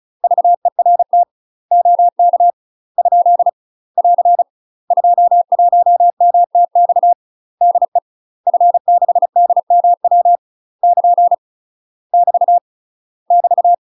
Morse code, heard as a keyed tone, about 36 wpm, 700 Hz. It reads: VERT OK ? <AR> 21MTX DE F6DMW <KN> <BT> <BT>
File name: UK MASTER QSOs TRAINING / 35WPM / UK MASTER QSOs 0009